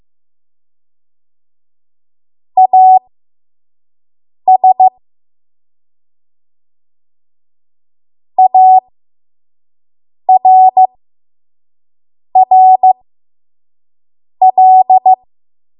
Morse code, heard as a keyed tone, about 15 wpm, spaced as for 5 wpm, 750 Hz. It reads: AS ARRL
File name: Morse / 05WPM